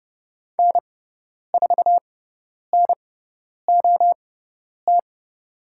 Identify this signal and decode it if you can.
Morse code, heard as a keyed tone, about 30 wpm, spaced as for 10 wpm, 700 Hz. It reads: N4NOT
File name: Lesson 18g - Letter m - Call Signs 30wpm